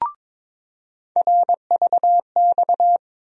Morse code, heard as a keyed tone, about 22 wpm, 700 Hz. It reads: RVX
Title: USA Call Sign Postfixes - 3 Letters 22wpm